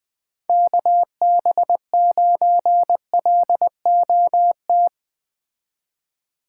Morse code, heard as a keyed tone, about 20 wpm, 700 Hz. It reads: KB9LOT
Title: Speed Racing - USA Call Signs 17wpm